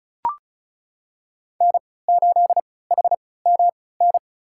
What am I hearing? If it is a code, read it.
Morse code, heard as a keyed tone, about 36 wpm, spaced as for 20 wpm, 700 Hz. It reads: N8HMN
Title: USA Call Signs - 5 Characters - 3x Character Spacing - 35wpm